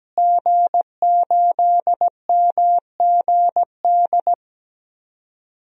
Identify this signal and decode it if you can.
Morse code, heard as a keyed tone, about 17 wpm, 700 Hz. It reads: G8MGD
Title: DXCC-100 Call Signs - 5 Characters - Repeated 2x - 17wpm